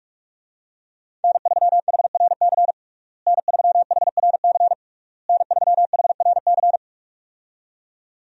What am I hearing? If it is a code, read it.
Morse code, heard as a keyed tone, about 47 wpm, 700 Hz. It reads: N3HRC N3HRC N3HRC
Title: USA Call Signs - 5 Characters - Repeated 3x - 45wpm